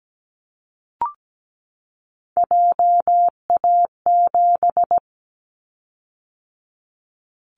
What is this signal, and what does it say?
Morse code, heard as a keyed tone, about 17 wpm, 700 Hz. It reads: JA7
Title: Callsign Prefixes for Top 25 DXCC - 17wpm